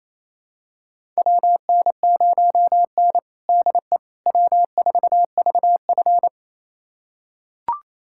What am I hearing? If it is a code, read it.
Morse code, heard as a keyed tone, about 28 wpm, 700 Hz. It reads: WN0N DE W4VF
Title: USA Callsign DE Callsign - 4 Characters - 28wpm